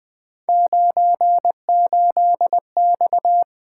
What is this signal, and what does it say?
Morse code, heard as a keyed tone, about 20 wpm, 700 Hz. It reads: 98X
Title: Three Letter-Number Combinations - 20wpm